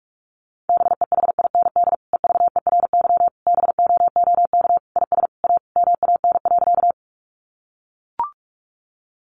Morse code, heard as a keyed tone, about 44 wpm, 700 Hz. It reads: BEHIND EVERY BOOK IS A MAN.
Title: Sentences from Top 500 Words 45wpm